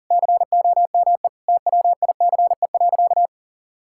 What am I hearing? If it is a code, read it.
Morse code, heard as a keyed tone, about 40 wpm, 700 Hz. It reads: COME TWICE.
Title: Sentences from Top 1600 Words 40wpm